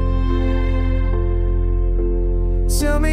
Dial tone (North America), 350 + 440 Hz.